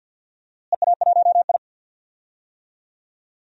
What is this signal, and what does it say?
Morse code, heard as a keyed tone, about 47 wpm, 700 Hz. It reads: EA1I
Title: DXCC-100 Call Signs - 4 Characters - Repeated 3x - 50wpm